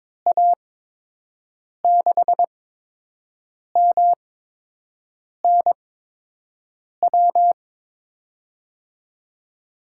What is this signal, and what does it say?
Morse code, heard as a keyed tone, about 22 wpm, spaced as for 6 wpm, 700 Hz. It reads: A6MNW
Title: USA Call Signs - 6 Characters - 8x Character Spacing - 22wpm